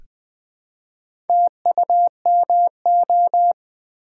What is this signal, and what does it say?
Morse code, heard as a keyed tone, about 20 wpm, 700 Hz. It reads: TUMO
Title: Build A Word - 1 Letter At A Time - 3 to 5-Letter Words - 20wpm